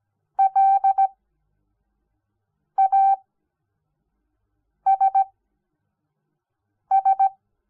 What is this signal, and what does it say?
Morse code, heard as a keyed tone, about 17 wpm, 780 Hz. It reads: L A S S
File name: Onlinelehrgang / morsen / DJ4UF Morsen06